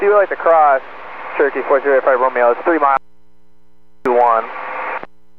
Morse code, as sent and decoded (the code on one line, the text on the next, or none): none